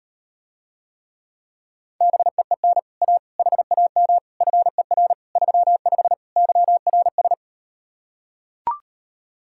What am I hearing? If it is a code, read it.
Morse code, heard as a keyed tone, about 38 wpm, 700 Hz. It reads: BEEN A HAM FER 35 YRS